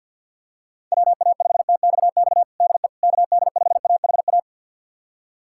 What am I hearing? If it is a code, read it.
Morse code, heard as a keyed tone, about 51 wpm, 700 Hz. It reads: WA5TXX DE KD5AHA
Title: USA Callsign DE Callsign - 6 Characters - 50wpm